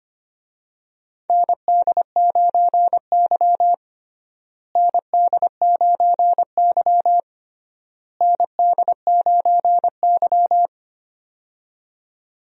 Morse code, heard as a keyed tone, about 25 wpm, 700 Hz. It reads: ND9Y ND9Y ND9Y
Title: USA Call Signs - 4 Characters - Repeated 3x - 25wpm